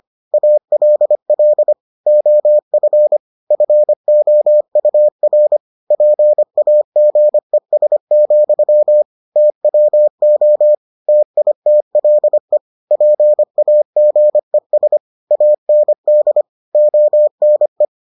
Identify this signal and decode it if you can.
Morse code, heard as a keyed tone, about 25 wpm, 600 Hz. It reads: ALL OF FOUR PAGES, TWO TITLE PAGES AND ONE